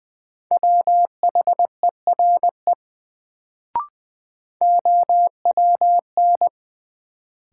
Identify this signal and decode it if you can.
Morse code, heard as a keyed tone, about 20 wpm, 700 Hz. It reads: WHERE OWN
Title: Sets of 2 Words - New Word Drill - 20wpm